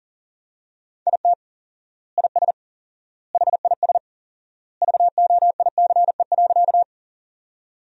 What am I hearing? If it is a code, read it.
Morse code, heard as a keyed tone, about 40 wpm, 700 Hz. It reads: IT IS HIS VOICE.
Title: 4x Word-Spacing - Sentences from Top 500 Words - 40wpm